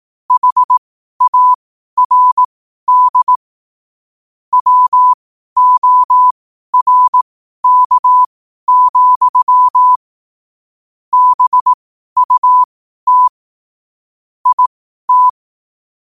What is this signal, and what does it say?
Morse code, heard as a keyed tone, about 18 wpm, spaced as for 12 wpm, 1000 Hz. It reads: HARD WORK, BUT IT
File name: qotdfiles / 12wpm